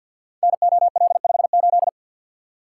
Morse code, heard as a keyed tone, about 46 wpm, 700 Hz. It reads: NOL58
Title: Code Groups - 5 Letter:Numbers - 50wpm